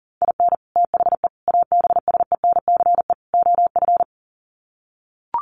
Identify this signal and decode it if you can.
Morse code, heard as a keyed tone, about 40 wpm, 700 Hz. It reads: IN THE ABSENCE OF
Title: Sets of 4 Words 40wpm